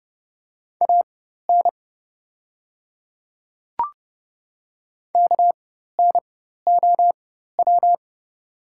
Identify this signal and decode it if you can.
Morse code, heard as a keyed tone, about 30 wpm, spaced as for 14 wpm, 690 Hz. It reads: AN KNOW
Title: 4x Character Spacing - Top 100 Words - 30wpm